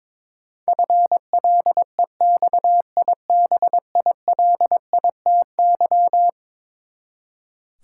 Morse code, heard as a keyed tone, about 22 wpm, 700 Hz. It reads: FLEXIBILITY
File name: practice / 22wpm / Eleven Letter Words - 22wpm